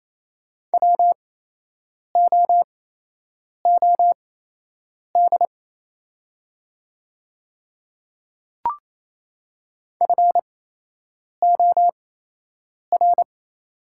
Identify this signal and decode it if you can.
Morse code, heard as a keyed tone, about 28 wpm, spaced as for 8 wpm, 700 Hz. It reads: WOOD FOR
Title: 8x Character Spacing - Top 300 Words - 28wpm